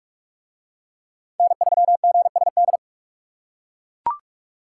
Morse code, heard as a keyed tone, about 42 wpm, 700 Hz. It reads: N3GSD